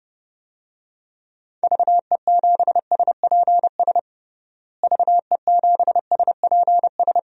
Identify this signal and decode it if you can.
Morse code, heard as a keyed tone, about 30 wpm, 700 Hz. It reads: VE7SPS VE7SPS
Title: DXCC-100 Call Signs - 6 Characters - Repeated 2x - 30wpm